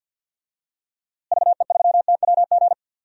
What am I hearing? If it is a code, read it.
Morse code, heard as a keyed tone, about 52 wpm, 700 Hz. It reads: VE3TWG
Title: DXCC-100 Call Signs - 6 Characters - Repeated 3x - 50wpm